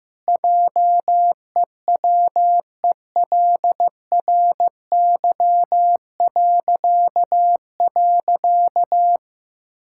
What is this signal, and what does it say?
Morse code, heard as a keyed tone, about 16 wpm, 700 Hz. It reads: JEWELRY..
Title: Pangram Challenge v2 - 15wpm